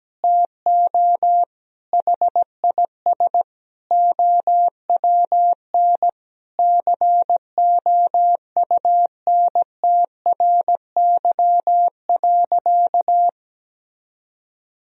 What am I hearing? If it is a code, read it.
Morse code, heard as a keyed tone, about 17 wpm, 700 Hz. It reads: TO HIS OWN COUNTRY.
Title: Sentences from Top 1000 Words - Review Part 1 - 17wpm